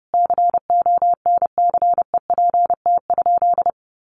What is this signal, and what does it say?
Morse code, heard as a keyed tone, about 30 wpm, 700 Hz. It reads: CONCEPT?
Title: Sentences from Top 1100 Words 30wpm